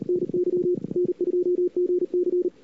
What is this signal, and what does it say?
Morse code, heard as a keyed tone, about 39 wpm, 400 Hz. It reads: NY N2GC
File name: cw-sprint-k5tr-op-n5rz